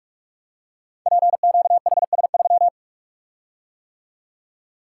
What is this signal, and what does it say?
Morse code, heard as a keyed tone, about 44 wpm, 690 Hz. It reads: PQHS3
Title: Code Groups - 5 Letter:Numbers - 45wpm